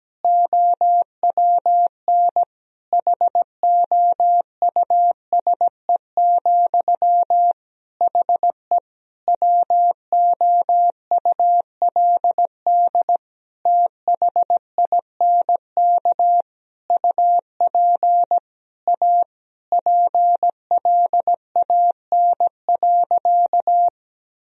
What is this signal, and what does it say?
Morse code, heard as a keyed tone, about 17 wpm, 700 Hz. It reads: OWN HOUSE, HE WOULD THINK UP A PLAN.
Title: Sentences from Top 1000 Words - Review Part 2 - 17wpm